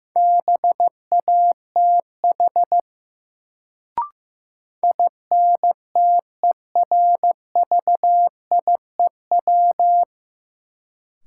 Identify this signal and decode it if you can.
Morse code, heard as a keyed tone, about 15 wpm, 700 Hz. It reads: BATH INTERVIEW
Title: Sentences from Top 1600 Words - New Word Drill Alt - 15wpm